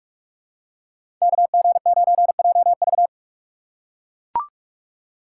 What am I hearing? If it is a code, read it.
Morse code, heard as a keyed tone, about 43 wpm, 700 Hz. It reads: KG9JV
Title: USA Call Signs - 6x Character Spacing - 45wpm